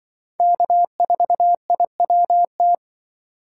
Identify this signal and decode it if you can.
Morse code, heard as a keyed tone, about 24 wpm, 700 Hz. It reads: K4IWT
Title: Speed Racing - USA 1x3 Call Signs 20wpm